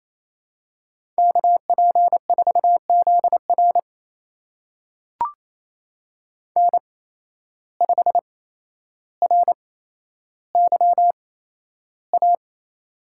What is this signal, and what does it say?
Morse code, heard as a keyed tone, about 28 wpm, spaced as for 12 wpm, 700 Hz. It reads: KP4ZR N5RYA
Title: USA Call Signs - 8x Character Spacing - 28wpm